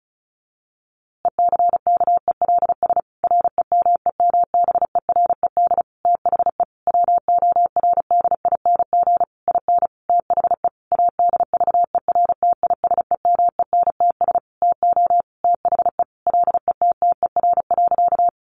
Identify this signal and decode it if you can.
Morse code, heard as a keyed tone, about 34 wpm, 700 Hz. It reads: ECKELS REMEMBERED THE WORDING IN THE ADVERTISEMENTS TO THE LETTER.
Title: A Sound Of Thunder 35wpm